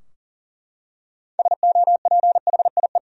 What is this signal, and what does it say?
Morse code, heard as a keyed tone, about 40 wpm, 700 Hz. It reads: SOPHIE
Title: USA Common Names - 40wpm